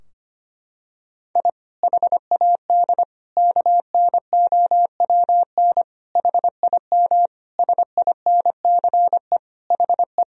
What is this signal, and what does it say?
Morse code, heard as a keyed tone, about 25 wpm, 700 Hz. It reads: I HAD KNOWN HIM SINCE HE
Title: Sentences from Top 1000 Words - Review Part 3 - 25wpm